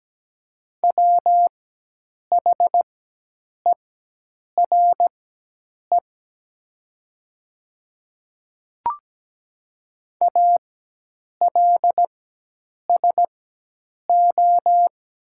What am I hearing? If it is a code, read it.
Morse code, heard as a keyed tone, about 17 wpm, spaced as for 8 wpm, 690 Hz. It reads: WHERE ALSO
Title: 4x Character Spacing - Top 200 Words - 17wpm